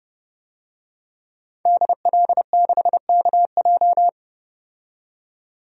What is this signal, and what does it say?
Morse code, heard as a keyed tone, about 29 wpm, 700 Hz. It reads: DL6KJ
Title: DXCC-100 Call Signs - 5 Characters - 30wpm